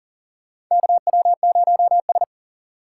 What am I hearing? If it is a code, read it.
Morse code, heard as a keyed tone, about 40 wpm, 700 Hz. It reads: KW0S
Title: USA Call Signs - 4 Characters - 4x Character Spacing - 40wpm